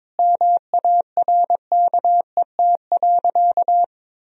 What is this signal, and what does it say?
Morse code, heard as a keyed tone, about 22 wpm, 700 Hz. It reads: MARKET.
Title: Sentences from Top 1000 Words - Review Part 3 - 22wpm